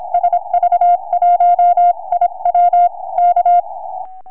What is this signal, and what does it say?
Morse code, heard as a keyed tone, about 27 wpm, 720 Hz. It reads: SV1IW K
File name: sv1iw 4m